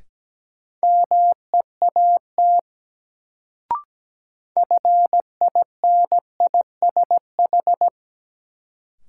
Morse code, heard as a keyed tone, about 17 wpm, 700 Hz. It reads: MEAT FINISH
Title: Sentences from Top 1200 Words - New Word Drill Alt - 17wpm